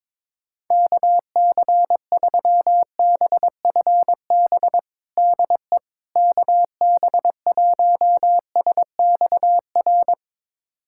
Morse code, heard as a keyed tone, about 21 wpm, 700 Hz. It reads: KC3BFB DE KB1SXR
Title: USA Callsign DE Callsign - 6 Characters - 22wpm